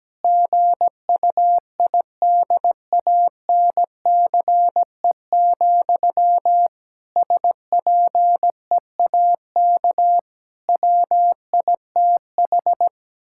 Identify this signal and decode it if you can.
Morse code, heard as a keyed tone, about 17 wpm, 700 Hz. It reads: GUIDANCE, SPEAK WITH
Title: Sentences from Top 1600 Words 17wpm